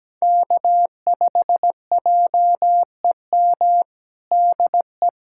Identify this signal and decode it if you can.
Morse code, heard as a keyed tone, about 17 wpm, 700 Hz. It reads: K5JEM DE
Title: USA Callsign DE Callsign 17wpm